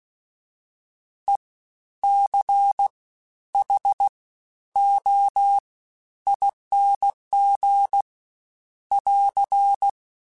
Morse code, heard as a keyed tone, about 16 wpm, 800 Hz. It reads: E C H O ING <AR>
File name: Special INGs-mp3s / Special INGs-mp3s / SlowToFaster ing @15